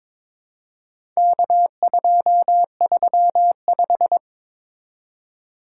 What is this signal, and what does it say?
Morse code, heard as a keyed tone, about 22 wpm, 700 Hz. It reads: K235
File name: practice / 22wpm / Four Letter-Number Combinations - 22wpm